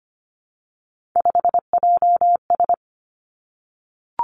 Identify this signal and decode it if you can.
Morse code, heard as a keyed tone, about 25 wpm, 700 Hz. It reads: 5JS
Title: Three Challenging Letter-Number Combinations 25wpm